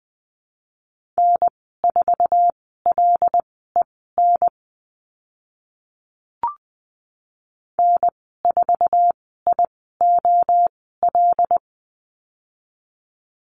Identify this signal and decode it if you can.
Morse code, heard as a keyed tone, about 20 wpm, spaced as for 14 wpm, 700 Hz. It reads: N4LEN N4IOL
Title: 076 Lesson 13g - Letter L - Call Signs 20wpm